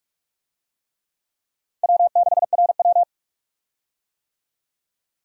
Morse code, heard as a keyed tone, about 42 wpm, 700 Hz. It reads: W6RW
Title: USA Call Signs - 4 Characters - 45wpm